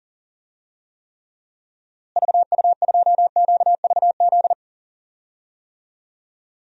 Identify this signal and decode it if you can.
Morse code, heard as a keyed tone, about 40 wpm, 700 Hz. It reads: VU2QVZ
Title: DXCC-100 Call Signs - 6 Characters - Repeated 3x - 40wpm